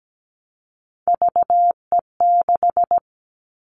Morse code, heard as a keyed tone, about 17 wpm, 700 Hz. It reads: VE6
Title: All Cut Numbers - 3-Digit 17wpm